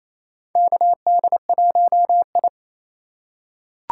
Morse code, heard as a keyed tone, about 28 wpm, 700 Hz. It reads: KD1I
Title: USA Call Signs - 3x Character Spacing - 28wpm